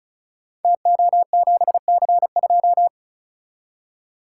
Morse code, heard as a keyed tone, about 36 wpm, 700 Hz. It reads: TO7C2
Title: Code Groups - 5 Letter:Numbers - 35wpm